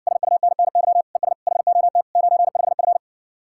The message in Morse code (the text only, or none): SUNNY ES HOT 95F